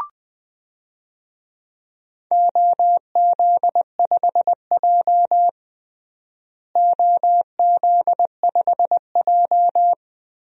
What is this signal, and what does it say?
Morse code, heard as a keyed tone, about 20 wpm, 700 Hz. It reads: OZ5J OZ5J
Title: DXCC-100 Call Signs - 4 Characters - Repeated 2x - 20wpm